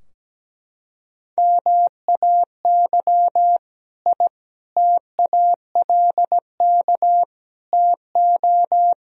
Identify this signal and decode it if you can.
Morse code, heard as a keyed tone, about 17 wpm, 700 Hz. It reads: MAY I TALK TO
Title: Sentences from Top 1000 Words - Review Part 4 - 17wpm